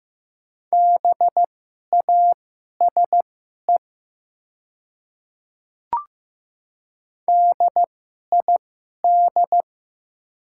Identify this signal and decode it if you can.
Morse code, heard as a keyed tone, about 15 wpm, spaced as for 11 wpm, 700 Hz. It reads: BASE DID
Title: 2x Character Spacing - Top 500 Words - 15wpm